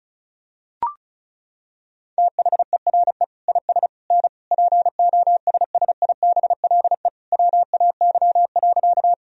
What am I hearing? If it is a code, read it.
Morse code, heard as a keyed tone, about 34 wpm, 700 Hz. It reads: THERE IS N POSSIBLE WAY.
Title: Sentences from Top 500 Words and Abbreviations 35wpm